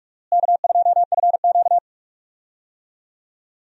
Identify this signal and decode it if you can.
Morse code, heard as a keyed tone, about 44 wpm, 700 Hz. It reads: K2FQ
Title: USA Call Signs - 4 Characters - Repeated 3x - 45wpm